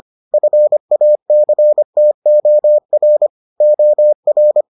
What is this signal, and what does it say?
Morse code, heard as a keyed tone, about 25 wpm, 600 Hz. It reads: FACTOR OR